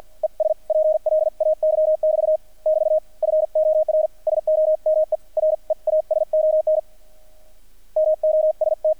Morse code, heard as a keyed tone, about 50 wpm, spaced as for 43 wpm, 620 Hz. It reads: ES PLAY <BT> <BT> FOR SOME REASON MOST
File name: archived / CRSnet